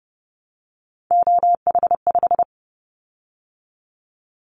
Morse code, heard as a keyed tone, about 32 wpm, 700 Hz. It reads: OH5
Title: Callsign Prefixes for Top 50 DXCC - 30wpm